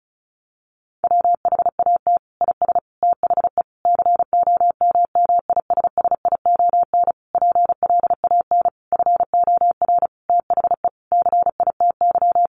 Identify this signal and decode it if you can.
Morse code, heard as a keyed tone, about 34 wpm, 700 Hz. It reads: WHAT IS THE COMMISSION PLAN FOR THE CITY
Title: Sentences from Top 1400 Words 35wpm